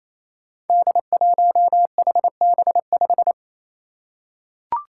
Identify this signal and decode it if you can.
Morse code, heard as a keyed tone, about 29 wpm, 700 Hz. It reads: D1HB5
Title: Five Challenging Letter-Number Combinations 28wpm